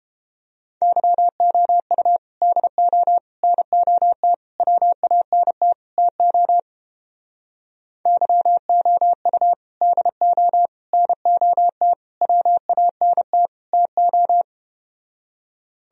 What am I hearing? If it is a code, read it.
Morse code, heard as a keyed tone, about 31 wpm, 700 Hz. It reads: YOU DO NOT WANT TO YOU DO NOT WANT TO